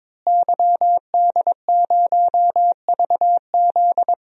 Morse code, heard as a keyed tone, about 21 wpm, 700 Hz. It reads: YD0VZ
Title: DXCC-100 Call Signs - 5 Characters - Repeated 2x - 22wpm